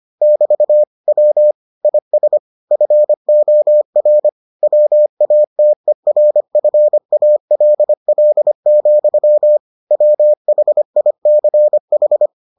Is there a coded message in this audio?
Yes: Morse code from 0.2 to 12.3 s, heard as a keyed tone, about 25 wpm, 600 Hz. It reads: <BT> W IS FOR WATERFALL, WHICH